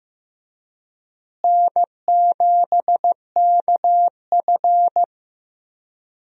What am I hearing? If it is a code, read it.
Morse code, heard as a keyed tone, about 15 wpm, 700 Hz. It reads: N7KF
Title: USA Call Signs - 4 Characters - Repeated 3x - 15wpm